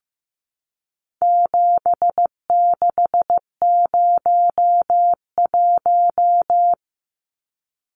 Morse code, heard as a keyed tone, about 15 wpm, 700 Hz. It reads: 7601
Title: Numbers - 4-Digit 15wpm